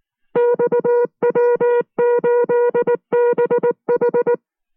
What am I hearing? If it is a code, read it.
Morse code, heard as a keyed tone, about 19 wpm, 480 Hz. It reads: XW8BH